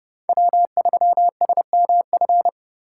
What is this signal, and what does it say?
Morse code, heard as a keyed tone, about 30 wpm, 700 Hz. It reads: W3SMF